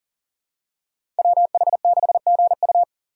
Morse code, heard as a keyed tone, about 40 wpm, 690 Hz. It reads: WH6GU
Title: USA Call Signs - 2x Character Spacing - 40wpm